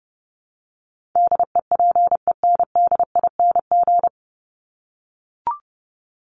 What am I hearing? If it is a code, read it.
Morse code, heard as a keyed tone, about 29 wpm, 700 Hz. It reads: DEPENDING
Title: ING Suffix - Top 100 Words with Base Words 30wpm